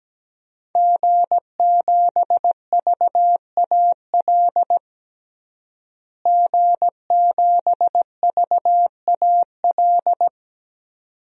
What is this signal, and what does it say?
Morse code, heard as a keyed tone, about 17 wpm, 700 Hz. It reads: G7VAL G7VAL
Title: DXCC-100 Call Signs - 5 Characters - Repeated 2x - 17wpm